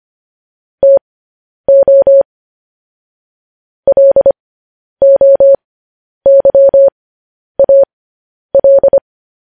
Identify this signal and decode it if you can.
Morse code, heard as a keyed tone, about 25 wpm, spaced as for 10 wpm, 550 Hz. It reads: TO LOYAL